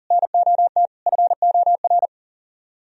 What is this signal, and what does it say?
Morse code, heard as a keyed tone, about 52 wpm, 700 Hz. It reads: NOT FOR